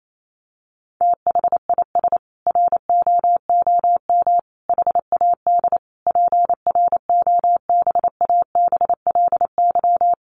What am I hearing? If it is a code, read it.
Morse code, heard as a keyed tone, about 28 wpm, 700 Hz. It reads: THIS ROOM HAD PROBABLY
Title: Sentences from Top 700 Words 28wpm